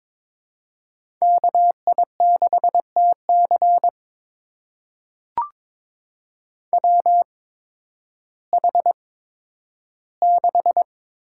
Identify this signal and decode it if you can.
Morse code, heard as a keyed tone, about 22 wpm, spaced as for 12 wpm, 700 Hz. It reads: KI6TC WH6
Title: USA Call Signs - 5 Characters - 8x Character Spacing - 22wpm